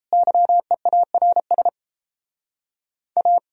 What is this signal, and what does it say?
Morse code, heard as a keyed tone, about 37 wpm, 700 Hz. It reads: YEARS A